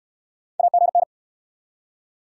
Morse code, heard as a keyed tone, about 66 wpm, 700 Hz. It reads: AGN